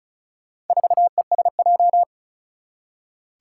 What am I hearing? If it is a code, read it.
Morse code, heard as a keyed tone, about 36 wpm, 700 Hz. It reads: 4ESJ
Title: Four Challenging Letter-Number Combinations 35wpm